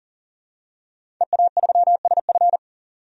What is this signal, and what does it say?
Morse code, heard as a keyed tone, about 37 wpm, 700 Hz. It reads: EA3SF
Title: DXCC-100 Call Signs - 5 Characters - 40wpm